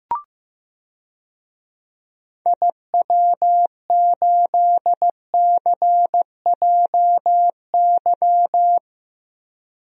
Morse code, heard as a keyed tone, about 15 wpm, 700 Hz. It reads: IW8CJY
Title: DXCC-100 Call Signs - 6 Characters - Repeated 2x - 15wpm